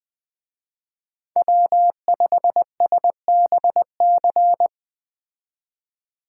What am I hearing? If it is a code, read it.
Morse code, heard as a keyed tone, about 20 wpm, 700 Hz. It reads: W5SBC